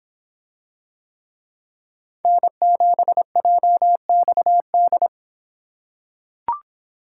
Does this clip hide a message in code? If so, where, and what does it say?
Yes: Morse code from 2.2 to 5.1 s, heard as a keyed tone, about 26 wpm, 700 Hz. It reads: N7JXD